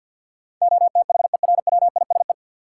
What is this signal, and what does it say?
Morse code, heard as a keyed tone, about 52 wpm, 700 Hz. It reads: OTHERWISE
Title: Sentences from Top 1200 Words - New Word Drill - 50wpm